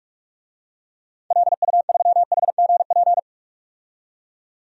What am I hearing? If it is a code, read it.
Morse code, heard as a keyed tone, about 42 wpm, 700 Hz. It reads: LU3HGP
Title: DXCC-100 Call Signs - 6 Characters - 45wpm